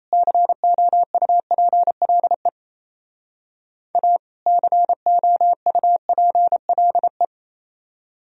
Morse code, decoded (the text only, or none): COUPLE A COUPLE